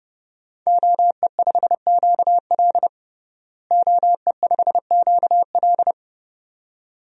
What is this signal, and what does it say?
Morse code, heard as a keyed tone, about 30 wpm, 700 Hz. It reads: OE5QL OE5QL
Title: DXCC-100 Call Signs - 5 Characters - Repeated 2x - 30wpm